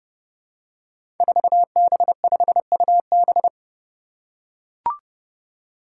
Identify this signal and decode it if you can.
Morse code, heard as a keyed tone, about 30 wpm, 700 Hz. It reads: 4B5UB
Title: Five Challenging Letter-Number Combinations 30wpm